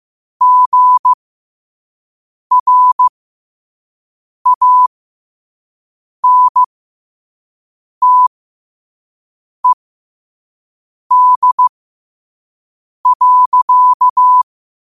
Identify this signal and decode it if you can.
Morse code, heard as a keyed tone, about 16 wpm, spaced as for 5 wpm, 1000 Hz. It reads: GRANTED.
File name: qotdfiles / 5wpm